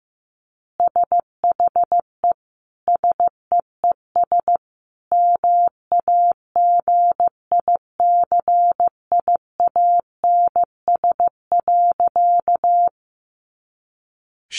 Morse code, heard as a keyed tone, about 15 wpm, 700 Hz. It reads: SHE SEES MAGICIANS.
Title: Easy Phrases 15wpm